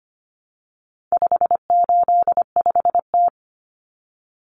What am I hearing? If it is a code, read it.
Morse code, heard as a keyed tone, about 25 wpm, 700 Hz. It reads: 585T
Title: Common Cut Numbers - 4-Digit 25wpm